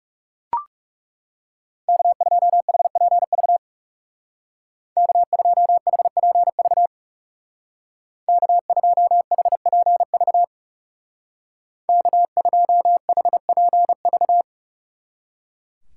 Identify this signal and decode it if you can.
Morse code, heard as a keyed tone, about 38 wpm, 700 Hz. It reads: K2HPV K2HPV K2HPV K2HPV